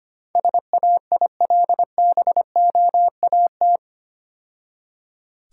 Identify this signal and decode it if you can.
Morse code, heard as a keyed tone, about 27 wpm, 700 Hz. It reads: SAILBOAT
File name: Easy Words - Speed-Racing with Hints - 25wpm